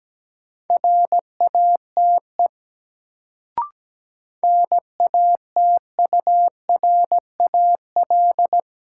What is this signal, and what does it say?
Morse code, heard as a keyed tone, about 17 wpm, 700 Hz. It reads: RATE NATURAL
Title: Sentences from Top 700 Words - New Word Drill - 17wpm